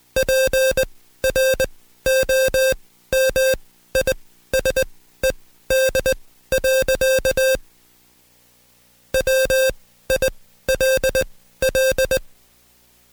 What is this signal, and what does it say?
Morse code, heard as a keyed tone, about 20 wpm, spaced as for 13 wpm, 530 Hz. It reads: PROMISED. WILL
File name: Code Quick / AudioDownld2 / Seg133-Stepper6 13WPM